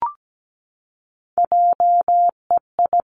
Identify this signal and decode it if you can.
Morse code, heard as a keyed tone, about 17 wpm, 700 Hz. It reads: JEI
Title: Three Challenging Letter-Number Combinations 17wpm